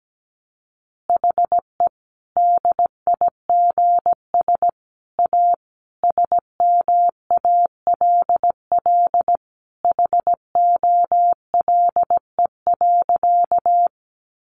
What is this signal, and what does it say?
Morse code, heard as a keyed tone, about 17 wpm, 700 Hz. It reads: HE DIGS A SMALL HOLE.
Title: Easy Phrases 17wpm